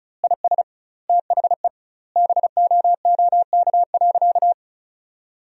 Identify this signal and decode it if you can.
Morse code, heard as a keyed tone, about 34 wpm, 700 Hz. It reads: IS THE BOOK.
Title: 2x Word-Spacing - Sentences from Top 300 Words - 35wpm